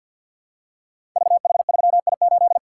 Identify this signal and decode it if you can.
Morse code, heard as a keyed tone, about 51 wpm, 700 Hz. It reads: VH3I8
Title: Code Groups - 5 Letter:Numbers - 50wpm